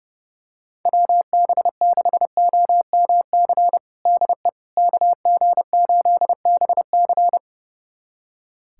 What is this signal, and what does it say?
Morse code, heard as a keyed tone, about 30 wpm, 700 Hz. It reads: WB6OMC DE KG8BC
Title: USA Callsign DE Callsign 30wpm